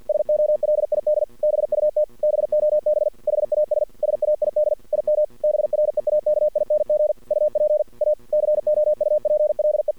Morse code, heard as a keyed tone, about 50 wpm, 600 Hz. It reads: SOLID BUT BOB HAS SAID IM BREAKING UP A COUPLE
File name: archived / CRSnet